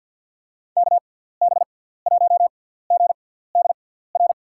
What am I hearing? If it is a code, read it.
Morse code, heard as a keyed tone, about 52 wpm, spaced as for 17 wpm, 700 Hz. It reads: KB1GDR
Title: USA Call Signs - 6 Characters - 6x Character Spacing - 50wpm